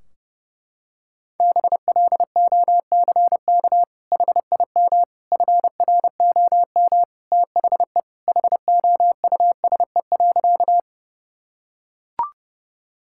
Morse code, heard as a keyed tone, about 30 wpm, 700 Hz. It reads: BLOCK HIM FROM THE HOUSE.
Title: Sentences from Top 1600 Words 30wpm